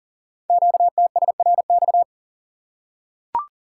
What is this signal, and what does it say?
Morse code, heard as a keyed tone, about 39 wpm, 700 Hz. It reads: QTSRX
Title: Code Groups - 5 Letters - 40wpm